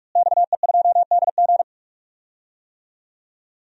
Morse code, heard as a keyed tone, about 41 wpm, 700 Hz. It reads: XE2DG
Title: DXCC-100 Call Signs - 5 Characters - Repeated 2x - 45wpm